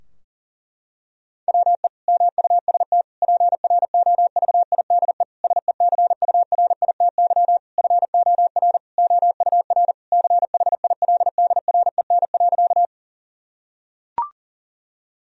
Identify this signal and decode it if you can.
Morse code, heard as a keyed tone, about 37 wpm, 700 Hz. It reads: WE MUST PROVIDE SECURITY FOR OUR CHILDREN.
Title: Sentences from Top 1000 Words - Review Part 4 - 40wpm